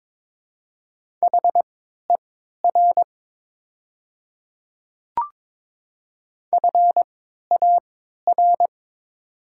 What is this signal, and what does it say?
Morse code, heard as a keyed tone, about 22 wpm, spaced as for 13 wpm, 700 Hz. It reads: HER FAR
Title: 3x Character Spacing - Top 500 Words - 22wpm